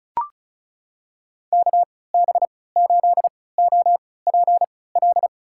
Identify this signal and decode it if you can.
Morse code, heard as a keyed tone, about 34 wpm, spaced as for 20 wpm, 700 Hz. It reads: KB8OPL
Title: USA Call Signs - 6 Characters - 3x Character Spacing - 35wpm